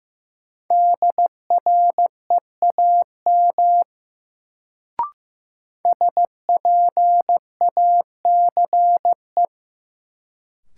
Morse code, heard as a keyed tone, about 15 wpm, 700 Hz. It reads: DREAM SPACE
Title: Sentences from Top 1500 Words - New Word Drill Alt - 15wpm